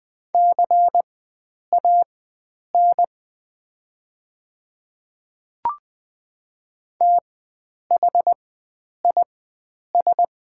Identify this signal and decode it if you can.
Morse code, heard as a keyed tone, about 20 wpm, spaced as for 9 wpm, 690 Hz. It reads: CAN THIS
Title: 4x Character Spacing - Top 100 Words - 20wpm